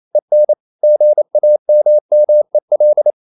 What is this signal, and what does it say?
Morse code, heard as a keyed tone, about 28 wpm, 600 Hz. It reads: EN GAMMEL
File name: Ferie QRQ 28wpm